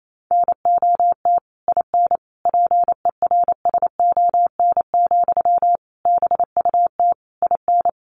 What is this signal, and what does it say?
Morse code, heard as a keyed tone, about 28 wpm, 700 Hz. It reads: NOT IN PERSON, BUT IN
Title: Sentences from Top 1400 Words 28wpm